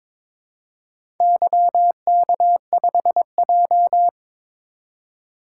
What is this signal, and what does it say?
Morse code, heard as a keyed tone, about 22 wpm, 700 Hz. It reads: YK5J
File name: practice / 22wpm / Four Letter-Number Combinations - 22wpm